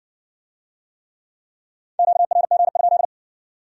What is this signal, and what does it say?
Morse code, heard as a keyed tone, about 57 wpm, 700 Hz. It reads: QRG?